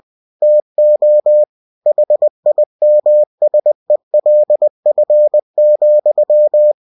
Morse code, heard as a keyed tone, about 20 wpm, 600 Hz. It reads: TO HIMSELF,
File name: cw-books / winnie-the-pooh / winnie-20wpm-page-0040